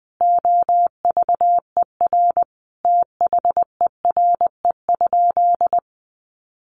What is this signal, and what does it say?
Morse code, heard as a keyed tone, about 20 wpm, 700 Hz. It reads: OVER THERE?
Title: Sentences from Top 1500 Words 20wpm